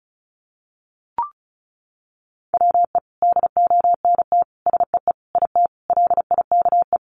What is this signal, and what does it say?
Morse code, heard as a keyed tone, about 34 wpm, 700 Hz. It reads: WE DONT SEE IT LIKE
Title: Sentences from Top 300 Words 35wpm